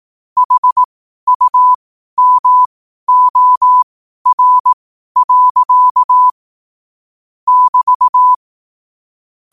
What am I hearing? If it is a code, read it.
Morse code, heard as a keyed tone, about 18 wpm, spaced as for 12 wpm, 1000 Hz. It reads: HUMOR. <BT>
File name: qotdfiles / 12wpm